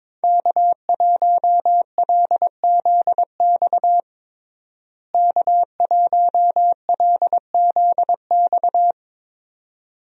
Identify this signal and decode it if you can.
Morse code, heard as a keyed tone, about 22 wpm, 700 Hz. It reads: K1LZX K1LZX
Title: USA Call Signs - 5 Characters - Repeated 2x - 22wpm